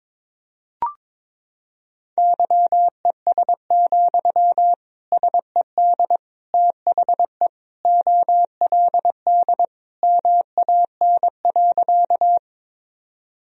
Morse code, heard as a keyed tone, about 22 wpm, 700 Hz. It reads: YES, SED THE OLD MAN.
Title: Sentences from Top 500 Words and Abbreviations 22wpm